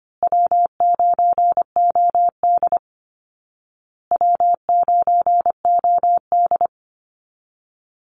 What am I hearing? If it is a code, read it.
Morse code, heard as a keyed tone, about 25 wpm, 700 Hz. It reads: W9OD W9OD
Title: Call Signs Repeated 2x - 25wpm